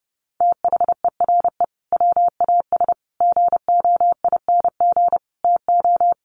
Morse code, heard as a keyed tone, about 30 wpm, 700 Hz. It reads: THERE WAS GOING TO
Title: Sets of 5 Words Encore2 30wpm